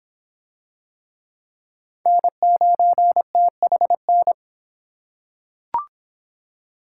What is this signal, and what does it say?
Morse code, heard as a keyed tone, about 26 wpm, 700 Hz. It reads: N9THN